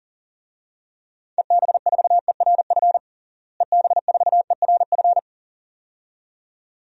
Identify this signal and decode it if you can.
Morse code, heard as a keyed tone, about 40 wpm, 700 Hz. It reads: EB4ERF EB4ERF
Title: DXCC-100 Call Signs - 6 Characters - Repeated 2x - 40wpm